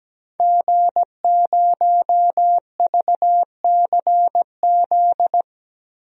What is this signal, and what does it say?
Morse code, heard as a keyed tone, about 17 wpm, 700 Hz. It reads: G0VCZ
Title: DXCC-100 Call Signs - 5 Characters - 17wpm